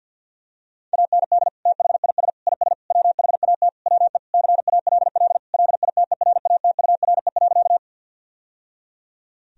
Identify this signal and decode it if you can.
Morse code, heard as a keyed tone, about 51 wpm, 700 Hz. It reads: AND THIS IS WHAT WE CALL LITERATURE.
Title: Sentences from Top 1000 Words - Review Part 3 - 50wpm